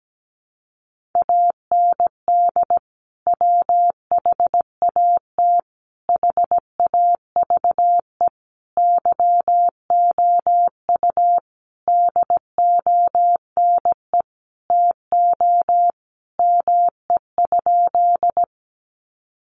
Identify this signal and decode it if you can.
Morse code, heard as a keyed tone, about 17 wpm, 700 Hz. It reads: AND WHAT HAVE YOU DONE TO ME?
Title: Sentences from Top 500 Words 17wpm